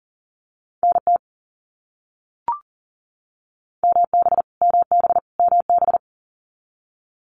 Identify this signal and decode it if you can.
Morse code, heard as a keyed tone, about 40 wpm, 700 Hz. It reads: NT MB MB MB
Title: Canadian Provinces and Territories - Repeated 3x - 40wpm